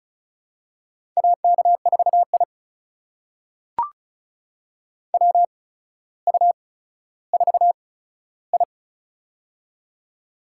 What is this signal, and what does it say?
Morse code, heard as a keyed tone, about 36 wpm, spaced as for 15 wpm, 700 Hz. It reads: AK4I WU4I
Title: USA Call Signs - 4 Characters - 8x Character Spacing - 35wpm